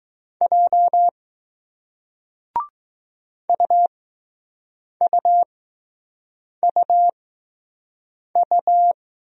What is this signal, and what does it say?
Morse code, heard as a keyed tone, about 20 wpm, spaced as for 6 wpm, 700 Hz. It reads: J UUUU